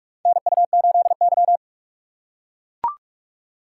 Morse code, heard as a keyed tone, about 45 wpm, 700 Hz. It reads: NU8Y
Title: USA Call Signs - 4x Character Spacing - 45wpm